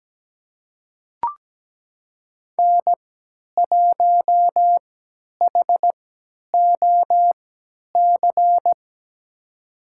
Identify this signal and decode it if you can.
Morse code, heard as a keyed tone, about 17 wpm, spaced as for 10 wpm, 700 Hz. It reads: N1HOC
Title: USA Call Signs - 5 Characters - 3x Character Spacing - 17wpm